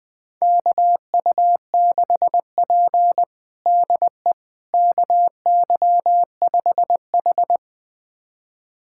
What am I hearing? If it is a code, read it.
Morse code, heard as a keyed tone, about 21 wpm, 700 Hz. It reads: KU6P DE KY5H
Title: USA Callsign DE Callsign - 4 Characters - 20wpm